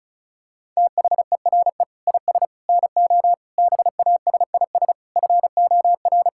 Morse code, heard as a keyed tone, about 34 wpm, 700 Hz. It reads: THERE IS NO BASIS FOR
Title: Sentences from Top 1600 Words 35wpm